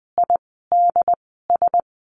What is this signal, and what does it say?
Morse code, heard as a keyed tone, about 20 wpm, spaced as for 14 wpm, 700 Hz. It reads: IDS
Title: 068 Lesson 12f - Letter d - Words 20wpm